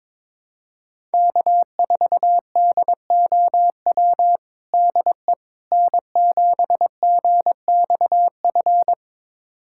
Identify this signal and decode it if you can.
Morse code, heard as a keyed tone, about 22 wpm, 700 Hz. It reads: K4DOW DE N7GXF
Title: USA Callsign DE Callsign - 5 Characters - 22wpm